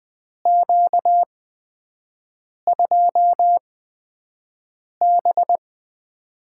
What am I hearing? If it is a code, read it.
Morse code, heard as a keyed tone, about 20 wpm, spaced as for 5 wpm, 700 Hz. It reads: Q2B
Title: USA Call Signs - 4 Characters - 8x Character Spacing - 20wpm